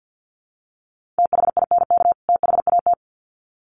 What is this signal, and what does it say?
Morse code, heard as a keyed tone, about 50 wpm, 700 Hz. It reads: THINK THAT
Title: Sets of 2 Words 50wpm